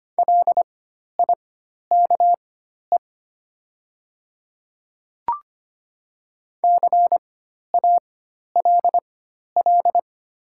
Morse code, heard as a keyed tone, about 25 wpm, spaced as for 12 wpm, 700 Hz. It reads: LIKE CALL
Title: 4x Character Spacing - Top 200 Words - 25wpm